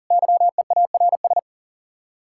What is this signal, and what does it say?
Morse code, heard as a keyed tone, about 52 wpm, 700 Hz. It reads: YEARS